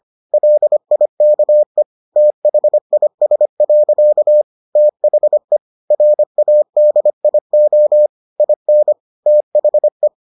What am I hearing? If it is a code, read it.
Morse code, heard as a keyed tone, about 25 wpm, 600 Hz. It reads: LIKE THIS. THE RADIO IN THE